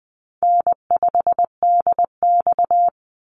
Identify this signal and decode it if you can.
Morse code, heard as a keyed tone, about 20 wpm, 700 Hz. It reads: N5DX
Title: Easy Call Signs Repeated 3x - 20wpm